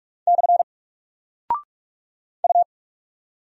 Morse code, heard as a keyed tone, about 44 wpm, 700 Hz. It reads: / U
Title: Single Character (Letter:Number:Punctuation:Prosign) Alt 45wpm